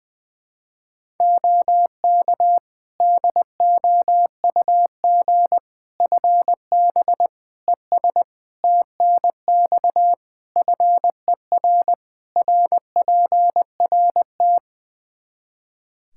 Morse code, heard as a keyed tone, about 20 wpm, 700 Hz. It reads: OK DOUG FB ES TNX FER RPRT
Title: Practice QSO - Exchange 2 - Info - 20wpm